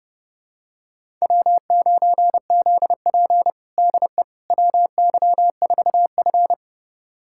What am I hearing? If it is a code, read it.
Morse code, heard as a keyed tone, about 32 wpm, 700 Hz. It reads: W9ZP DE WY4F
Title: USA Callsign DE Callsign - 4 Characters - 30wpm